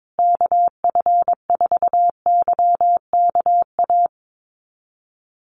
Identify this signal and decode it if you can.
Morse code, heard as a keyed tone, about 22 wpm, 700 Hz. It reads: KF4YKA
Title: USA Call Signs Repeated 2x 22wpm